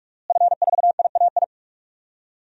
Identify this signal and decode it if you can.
Morse code, heard as a keyed tone, about 42 wpm, 700 Hz. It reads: F4IAI
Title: DXCC-100 Call Signs - 5 Characters - Repeated 3x - 45wpm